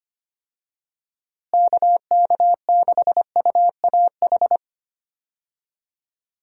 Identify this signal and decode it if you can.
Morse code, heard as a keyed tone, about 25 wpm, 700 Hz. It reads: KK6UAH
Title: USA Call Signs - 6 Characters - Repeated 3x - 25wpm